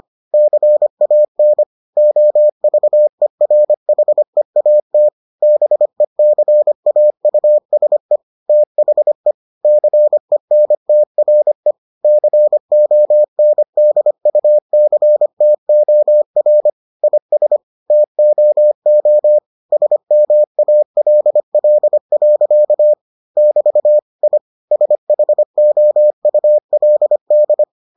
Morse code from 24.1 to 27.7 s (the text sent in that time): I SHOULD